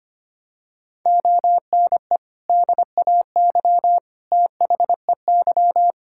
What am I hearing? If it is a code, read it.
Morse code, heard as a keyed tone, about 25 wpm, 700 Hz. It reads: ONE DAY THEY
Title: Sentences from Top 1600 Words 25wpm